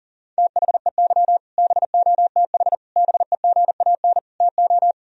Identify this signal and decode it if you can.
Morse code, heard as a keyed tone, about 40 wpm, 700 Hz. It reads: THEY BOTH BEGAN TO
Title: Sentences from Top 1000 Words - Review Part 1 - 40wpm